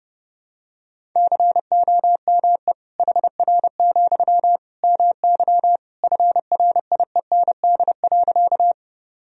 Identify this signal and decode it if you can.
Morse code, heard as a keyed tone, about 30 wpm, 700 Hz. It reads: COME HR, MY FRIEND.
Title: Sentences from Top 300 Words and Abbreviations 30wpm